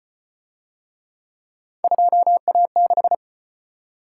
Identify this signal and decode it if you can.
Morse code, heard as a keyed tone, about 35 wpm, 700 Hz. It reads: 2A6